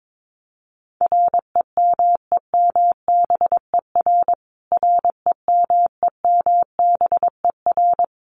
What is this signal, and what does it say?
Morse code, heard as a keyed tone, about 22 wpm, 700 Hz. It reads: REMEMBER REMEMBER
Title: Top 500 Words Repeated 22wpm